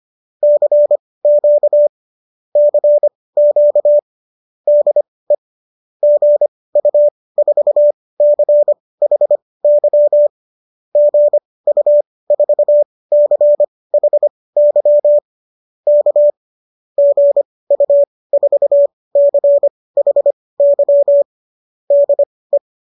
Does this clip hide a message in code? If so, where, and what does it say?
Yes: Morse code from 0.4 to 22.6 s, heard as a keyed tone, about 25 wpm, spaced as for 18 wpm, 570 Hz. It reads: CQ CQ DE GU4CHY GU4CHY K GU4CHY DE